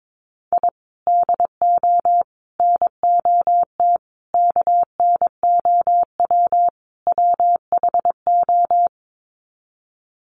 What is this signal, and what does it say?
Morse code, heard as a keyed tone, about 22 wpm, 700 Hz. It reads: I DO NOT KNOW WHO
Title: Sets of 5 Words 22wpm